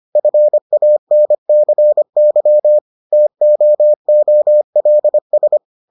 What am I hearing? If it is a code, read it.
Morse code, heard as a keyed tone, about 25 wpm, 600 Hz. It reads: FANCY TOOLS